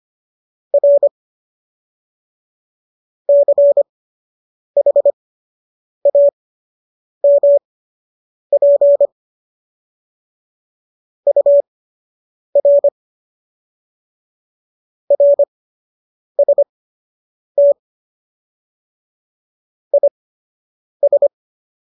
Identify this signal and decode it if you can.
Morse code, heard as a keyed tone, about 25 wpm, spaced as for 8 wpm, 570 Hz. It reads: R CHAMP UR RST IS